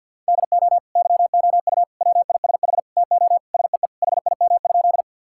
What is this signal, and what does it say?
Morse code, heard as a keyed tone, about 50 wpm, 700 Hz. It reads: DO YOU WISH TO SEE HIM?
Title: Sentences from Top 1600 Words 50wpm